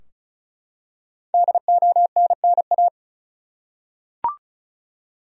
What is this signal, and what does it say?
Morse code, heard as a keyed tone, about 34 wpm, 700 Hz. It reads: DONNA
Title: USA Common Names - 35wpm